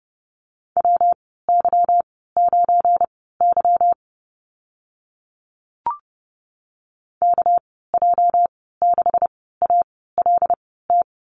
Morse code, heard as a keyed tone, about 30 wpm, spaced as for 17 wpm, 700 Hz. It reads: WY9Y KJ6ALT
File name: Lesson 37g - Letter x - Call Signs 30wpm